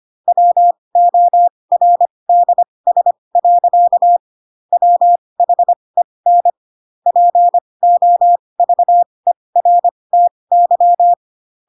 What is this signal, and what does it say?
Morse code, heard as a keyed tone, about 25 wpm, spaced as for 20 wpm, 700 Hz. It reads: WORDS. WHEN POVERTY
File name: Pinocchio / Pinocchio ch8 15 25 ik1hge